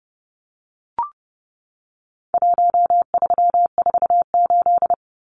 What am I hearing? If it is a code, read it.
Morse code, heard as a keyed tone, about 30 wpm, 700 Hz. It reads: 1348
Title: Numbers - Contest Style - 4-Digit 30wpm